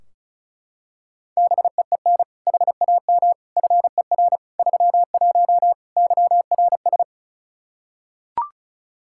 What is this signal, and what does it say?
Morse code, heard as a keyed tone, about 36 wpm, 700 Hz. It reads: BEEN HAM FER 31 YRS
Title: K1USN SST QSO Practice 35wpm